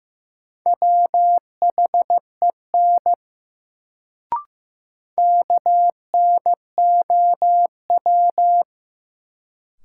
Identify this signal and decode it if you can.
Morse code, heard as a keyed tone, about 15 wpm, 700 Hz. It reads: WHEN KNOW
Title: Sentences from Top 100 Words - New Word Drill - 15wpm